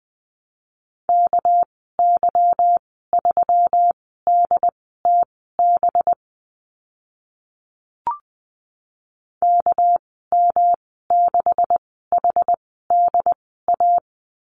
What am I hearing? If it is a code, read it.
Morse code, heard as a keyed tone, about 20 wpm, spaced as for 14 wpm, 700 Hz. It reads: KY3DTB KM6HDA
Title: 245 Lesson 39g - Letter z - Call Signs 20wpm